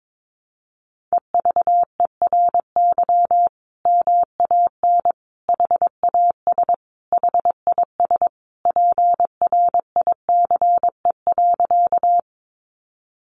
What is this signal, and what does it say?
Morse code, heard as a keyed tone, about 22 wpm, 700 Hz. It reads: EVERY MAN HAS HIS PRICE.
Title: Sentences from Top 600 Words 22wpm